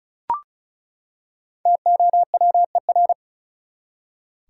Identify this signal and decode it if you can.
Morse code, heard as a keyed tone, about 36 wpm, 700 Hz. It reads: TOWER
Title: Five Letter Words - 35wpm